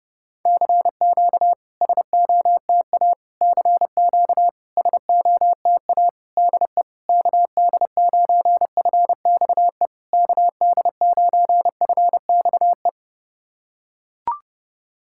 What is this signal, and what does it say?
Morse code, heard as a keyed tone, about 30 wpm, 690 Hz. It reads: CQ SOTA CQ SOTA DE KD9FXE KD9FXE
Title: SOTA QSO Practice 30wpm